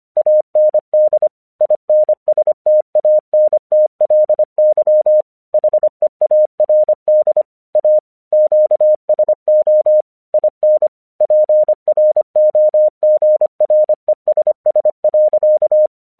Morse code, heard as a keyed tone, about 24 wpm, 610 Hz. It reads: AND INSTANTLY HEARD A QSO IN PROGRESS.
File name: 1 English 25wpm / AQC28